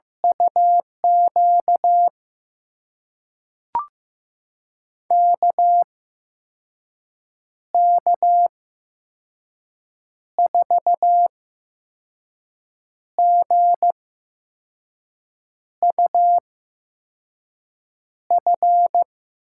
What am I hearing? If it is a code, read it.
Morse code, heard as a keyed tone, about 15 wpm, spaced as for 5 wpm, 700 Hz. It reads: UQ KK4GUF
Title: USA Call Signs - 6 Characters - 8x Character Spacing - 15wpm